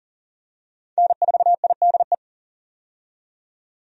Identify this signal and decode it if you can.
Morse code, heard as a keyed tone, about 40 wpm, 700 Hz. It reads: N4IDE
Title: USA Call Signs - 5 Characters - 40wpm